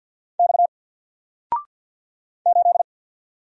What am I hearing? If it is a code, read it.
Morse code, heard as a keyed tone, about 50 wpm, 700 Hz. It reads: X 8
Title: Single Letter-Number 50wpm